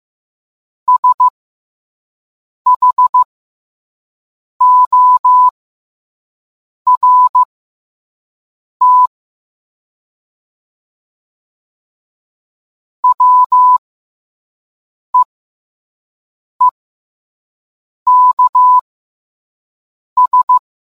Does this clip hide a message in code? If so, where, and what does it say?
Yes: Morse code from 0.9 to 20.6 s, heard as a keyed tone, about 15 wpm, spaced as for 5 wpm, 1000 Hz. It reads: SHORT WEEKS